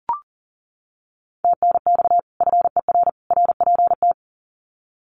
Morse code, heard as a keyed tone, about 40 wpm, 700 Hz. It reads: TNX FER RPT
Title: Common QSO Phrases 40wpm